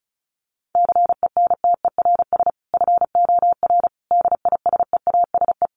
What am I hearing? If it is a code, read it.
Morse code, heard as a keyed tone, about 34 wpm, 700 Hz. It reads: CENTERS FOR DISEASE
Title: Sets of 5 Words Encore2 35wpm